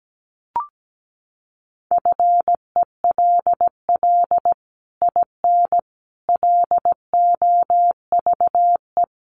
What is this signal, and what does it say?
Morse code, heard as a keyed tone, about 17 wpm, 700 Hz. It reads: FELL IN LOVE
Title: Sets of 5 Words Encore2 17wpm